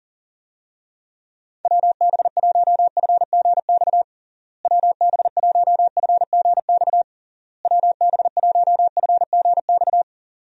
Morse code, heard as a keyed tone, about 37 wpm, 700 Hz. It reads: WB1FGX WB1FGX WB1FGX
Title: USA Call Signs - 6 Characters - Repeated 3x - 40wpm